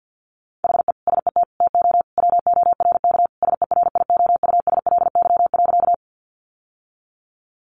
Morse code, heard as a keyed tone, about 50 wpm, 700 Hz. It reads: HE SET TO WORK SERIOUSLY.
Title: Sentences from Top 1200 Words 50wpm